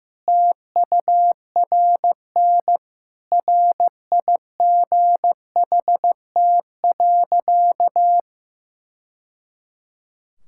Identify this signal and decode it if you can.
Morse code, heard as a keyed tone, about 15 wpm, 700 Hz. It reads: TURN RIGHT.
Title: Sentences from Top 1000 Words - Review Part 5 - 15wpm